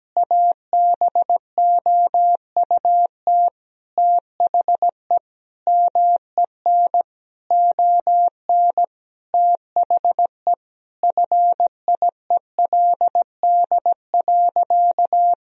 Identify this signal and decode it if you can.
Morse code, heard as a keyed tone, about 17 wpm, 700 Hz. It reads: ABOUT THE MEN ON THE FIELD.
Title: Sentences from Top 1000 Words - Review Part 2 - 17wpm